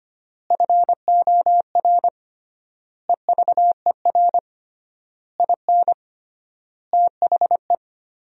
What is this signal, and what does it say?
Morse code, heard as a keyed tone, about 25 wpm, 700 Hz. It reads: FOR EVER IN THE
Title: 3x Word-Spacing - Sentences from Top 400 Words - 25wpm